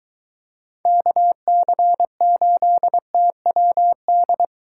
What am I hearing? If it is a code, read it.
Morse code, heard as a keyed tone, about 23 wpm, 700 Hz. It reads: KC8TWD